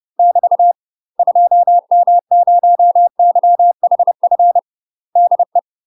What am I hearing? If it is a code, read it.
Morse code, heard as a keyed tone, about 30 wpm, 700 Hz. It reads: <BT> 2M0YHF DE
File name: UK MASTER QSOs TRAINING / 30WPM / UK MASTER QSOs 0013